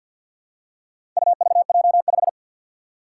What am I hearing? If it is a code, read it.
Morse code, heard as a keyed tone, about 51 wpm, 700 Hz. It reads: UVJ5
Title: Four Challenging Letter-Number Combinations 50wpm